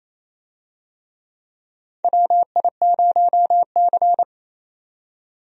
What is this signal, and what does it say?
Morse code, heard as a keyed tone, about 27 wpm, 700 Hz. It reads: WI0C
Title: USA Call Signs - 4 Characters - 28wpm